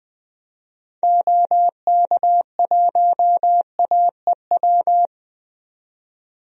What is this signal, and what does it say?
Morse code, heard as a keyed tone, about 20 wpm, 700 Hz. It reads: OK1AEW
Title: DXCC-100 Call Signs - 6 Characters - 20wpm